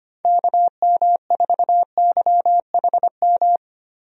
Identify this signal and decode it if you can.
Morse code, heard as a keyed tone, about 25 wpm, 700 Hz. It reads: KM4YHM
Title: USA Call Signs - 6 Characters - Repeated 3x - 25wpm